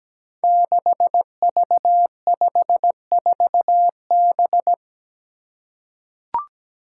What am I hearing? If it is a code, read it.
Morse code, heard as a keyed tone, about 17 wpm, 700 Hz. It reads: 6V54B
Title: Five Challenging Letter-Number Combinations 17wpm